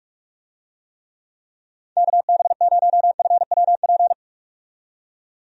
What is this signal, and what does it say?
Morse code, heard as a keyed tone, about 46 wpm, 700 Hz. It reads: KB0FWP
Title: USA Call Signs - 6 Characters - 45wpm